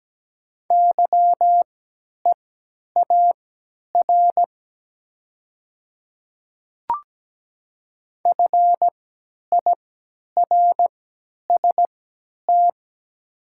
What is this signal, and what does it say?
Morse code, heard as a keyed tone, about 17 wpm, spaced as for 10 wpm, 700 Hz. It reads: YEAR FIRST
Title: 3x Character Spacing - Top 300 Words - 17wpm